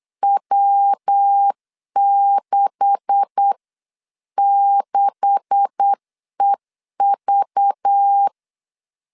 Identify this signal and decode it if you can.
Morse code, heard as a keyed tone, about 8 wpm, 800 Hz. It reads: W66EV
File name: LEZIONI / 9 286 8wpm 3425